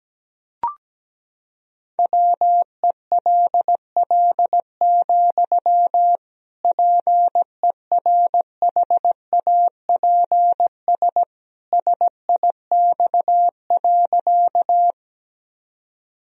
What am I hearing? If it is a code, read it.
Morse code, heard as a keyed tone, about 17 wpm, 700 Hz. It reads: WELL, PERHAPS SIX.
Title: Sentences from Top 1000 Words - Review Part 2 - 17wpm